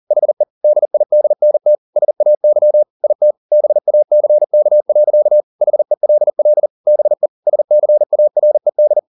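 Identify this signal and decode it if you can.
Morse code, heard as a keyed tone, about 40 wpm, 600 Hz. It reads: HE DIDNT SAY IT BACK. HELL BE SCARED
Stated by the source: Morse code at 40 wpm, practice text, CW med Gnister.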